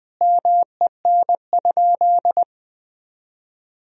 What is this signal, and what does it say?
Morse code, heard as a keyed tone, about 21 wpm, 700 Hz. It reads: MEN?
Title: Sentences from Top 1000 Words - Review Part 1 - 20wpm